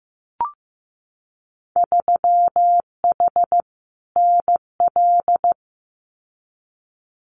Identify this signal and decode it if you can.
Morse code, heard as a keyed tone, about 15 wpm, 700 Hz. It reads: 3H NL
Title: Winter Field Day Exchanges Encore 15wpm